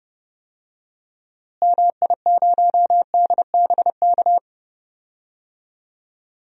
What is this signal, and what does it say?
Morse code, heard as a keyed tone, about 30 wpm, 700 Hz. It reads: MI0DBK
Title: DXCC-100 Call Signs - 6 Characters - Repeated 2x - 30wpm